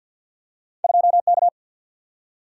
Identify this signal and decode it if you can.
Morse code, heard as a keyed tone, about 47 wpm, 700 Hz. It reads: 2 K
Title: Single-Letter Single-Number 50wpm